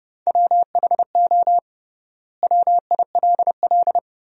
Morse code, heard as a keyed tone, about 30 wpm, 700 Hz. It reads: WHO WILL
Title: 3x Word-Spacing - Sentences from Top 100 Words - 30wpm